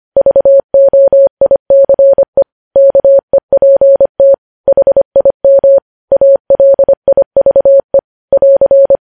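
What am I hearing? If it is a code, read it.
Morse code, heard as a keyed tone, about 25 wpm, 550 Hz. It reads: VOICE KEPT HIM ALIVE <AR>
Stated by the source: Morse code practice of news headlines at 25 wpm, opening with vvv.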